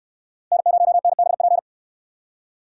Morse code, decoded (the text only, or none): N0NBP